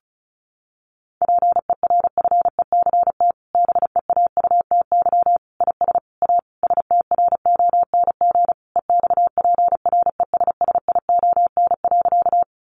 Morse code, heard as a keyed tone, about 34 wpm, 700 Hz. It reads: PERFECT BEAUTY IS A STRONG EXPRESSION.
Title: Sentences from Top 1200 Words 35wpm